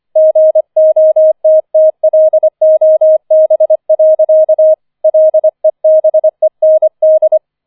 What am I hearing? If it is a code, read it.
Morse code, heard as a keyed tone, about 25 wpm, 630 Hz. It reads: GOTTLOB. LEBEND
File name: 004 Vierter Streich 25wpm